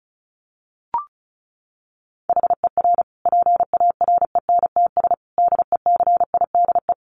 Morse code, heard as a keyed tone, about 34 wpm, 700 Hz. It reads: HER PARENTS DECIDE
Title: Easy Phrases 35wpm